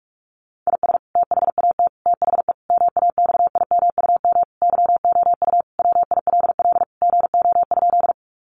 Morse code, heard as a keyed tone, about 44 wpm, 700 Hz. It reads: IS THAT THE MAXIMUM YOU WILL GO?
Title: Sentences from Top 1300 Words 45wpm